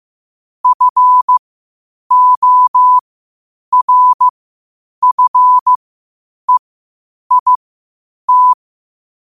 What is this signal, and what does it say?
Morse code, heard as a keyed tone, about 15 wpm, spaced as for 8 wpm, 1000 Hz. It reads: FORFEIT